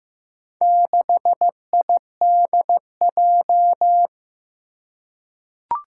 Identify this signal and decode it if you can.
Morse code, heard as a keyed tone, about 15 wpm, 700 Hz. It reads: 6IDJ
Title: Four Challenging Letter-Number Combinations 15wpm